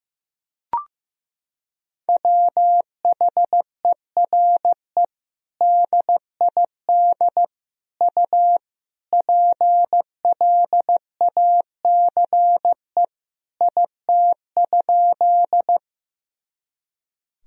Morse code, heard as a keyed tone, about 15 wpm, 700 Hz. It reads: WHERE DID U PLACE IT?
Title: Sentences from Top 200 Words and Abbreviations 15wpm